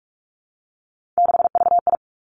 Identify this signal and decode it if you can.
Morse code, heard as a keyed tone, about 46 wpm, 700 Hz. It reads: 6VI
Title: Three Challenging Letter-Number Combinations 45wpm